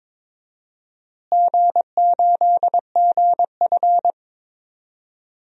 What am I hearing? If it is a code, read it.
Morse code, heard as a keyed tone, about 21 wpm, 700 Hz. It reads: G8GF
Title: DXCC-100 Call Signs - 4 Characters - Repeated 2x - 22wpm